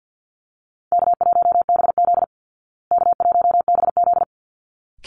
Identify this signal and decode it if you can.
Morse code, heard as a keyed tone, about 50 wpm, 700 Hz. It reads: K1BZ K1BZ
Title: Call Signs Repeated 3x - 50wpm